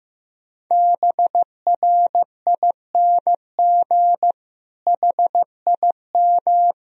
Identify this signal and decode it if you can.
Morse code, heard as a keyed tone, about 15 wpm, 700 Hz. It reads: BRING HIM
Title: Sentences from Top 1000 Words - Review Part 1 - 15wpm